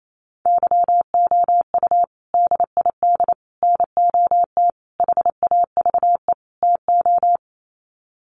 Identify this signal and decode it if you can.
Morse code, heard as a keyed tone, about 28 wpm, 700 Hz. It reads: YOU DID NOT HAVE TO
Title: Sets of 5 Words Encore 28wpm